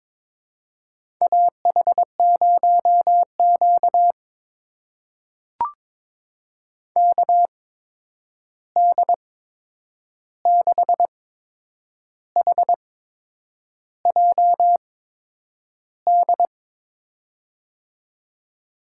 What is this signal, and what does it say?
Morse code, heard as a keyed tone, about 21 wpm, spaced as for 8 wpm, 700 Hz. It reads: AH0Q KD6HJD
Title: USA Call Signs - 8x Character Spacing - 22wpm